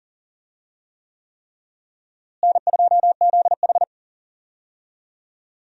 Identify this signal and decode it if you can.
Morse code, heard as a keyed tone, about 40 wpm, 700 Hz. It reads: N2ZH
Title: USA Call Signs - 4 Characters - Repeated 3x - 40wpm